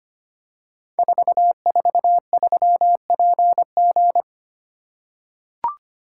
Morse code, heard as a keyed tone, about 25 wpm, 700 Hz. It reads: 443PG
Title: Code Groups - 5 Letter:Numbers - 25wpm